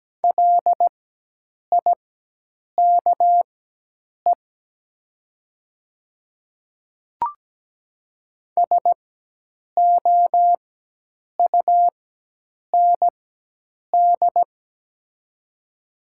Morse code, heard as a keyed tone, about 17 wpm, spaced as for 8 wpm, 700 Hz. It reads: LIKE SOUND
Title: 4x Character Spacing - Top 200 Words - 17wpm